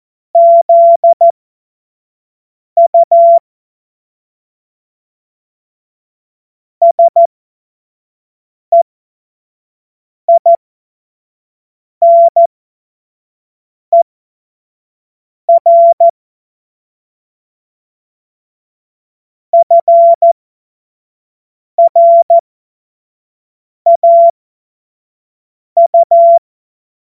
Morse code, heard as a keyed tone, about 14 wpm, spaced as for 5 wpm, 680 Hz. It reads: ZU SEINER FRAU
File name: Morsekurs / Witze / Witz 27. Dezember 2022 25 F70